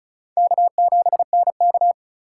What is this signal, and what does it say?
Morse code, heard as a keyed tone, about 36 wpm, 700 Hz. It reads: K7NK
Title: USA Call Signs - 4 Characters - 3x Character Spacing - 35wpm